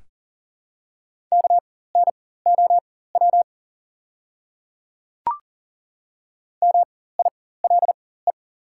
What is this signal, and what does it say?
Morse code, heard as a keyed tone, about 40 wpm, spaced as for 19 wpm, 700 Hz. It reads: KNOW MILE
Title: 4x Character Spacing - Top 500 Words - 40wpm